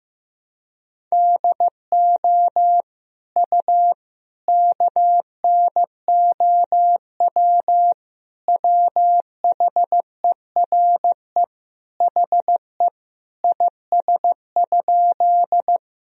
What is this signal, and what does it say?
Morse code, heard as a keyed tone, about 16 wpm, 700 Hz. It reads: DO U KNOW WHERE HE IS?
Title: Sentences from Top 300 Words and Abbreviations 15wpm